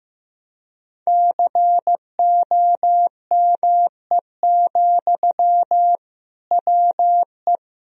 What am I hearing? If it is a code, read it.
Morse code, heard as a keyed tone, about 15 wpm, 700 Hz. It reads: COME, WE
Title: Sentences from Top 1000 Words - Review Part 1 - 15wpm